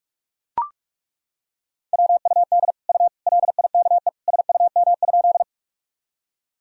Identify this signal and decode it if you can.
Morse code, heard as a keyed tone, about 41 wpm, 700 Hz. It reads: WUD U LIKE SUM?
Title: Sentences from Top 100 Words and Abbreviations 45wpm